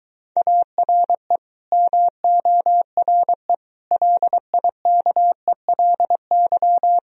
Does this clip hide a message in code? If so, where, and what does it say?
Yes: Morse code from 0.4 to 7.0 s, heard as a keyed tone, about 24 wpm, 700 Hz. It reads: ARE MORE LIKELY